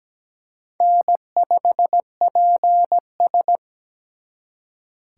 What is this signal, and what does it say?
Morse code, heard as a keyed tone, about 20 wpm, 700 Hz. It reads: N5PS
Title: Speed Racing - USA 1x2 Call Signs 17wpm